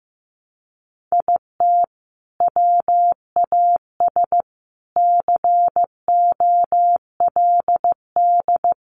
Morse code, heard as a keyed tone, about 15 wpm, 700 Hz. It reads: IT WAS COLD
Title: Sentences from Top 500 Words 15wpm